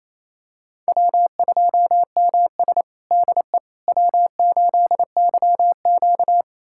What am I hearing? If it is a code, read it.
Morse code, heard as a keyed tone, about 28 wpm, 700 Hz. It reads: W2MS DE W8YQ
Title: USA Callsign DE Callsign - 4 Characters - 28wpm